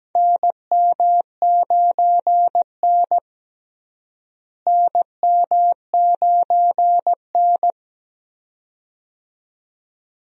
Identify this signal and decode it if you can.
Morse code, heard as a keyed tone, about 17 wpm, 700 Hz. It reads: NM9N NM9N
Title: USA Call Signs - 4 Characters - Repeated 2x - 17wpm